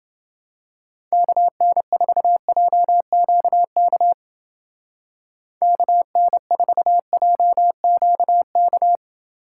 Morse code, heard as a keyed tone, about 28 wpm, 700 Hz. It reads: KN4JQK KN4JQK